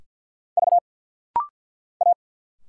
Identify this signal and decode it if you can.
Morse code, heard as a keyed tone, about 50 wpm, 700 Hz. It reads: V A
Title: Single Letters - Rapid-Fire 50wpm